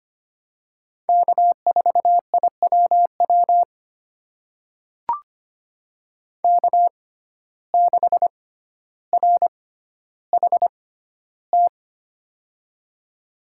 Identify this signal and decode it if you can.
Morse code, heard as a keyed tone, about 25 wpm, spaced as for 13 wpm, 700 Hz. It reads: K4IWW K6RHT
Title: USA Call Signs - 5 Characters - 6x Character Spacing - 25wpm